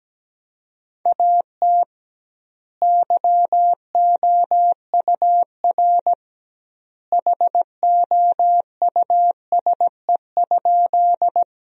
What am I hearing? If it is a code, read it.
Morse code, heard as a keyed tone, about 17 wpm, 700 Hz. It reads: AT YOUR HOUSE?
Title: 2x Word-Spacing - Sentences from Top 200 Words - 17wpm